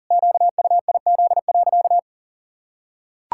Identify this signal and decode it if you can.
Morse code, heard as a keyed tone, about 40 wpm, 700 Hz. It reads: QUIZ.
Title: Pangram Challenge v2 - 40wpm